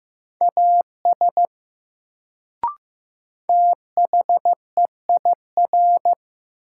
Morse code, heard as a keyed tone, about 15 wpm, 700 Hz. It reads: AS THEIR
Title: Sets of 2 Words - New Word Drill - 15wpm